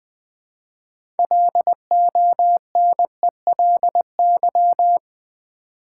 Morse code, heard as a keyed tone, about 21 wpm, 700 Hz. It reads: LONELY
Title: Six Letter Words - 20wpm